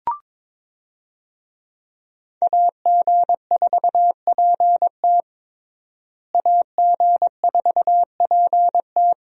Morse code, heard as a keyed tone, about 22 wpm, 700 Hz. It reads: AG4PT AG4PT
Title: USA Call Signs - 5 Characters - Repeated 2x - 22wpm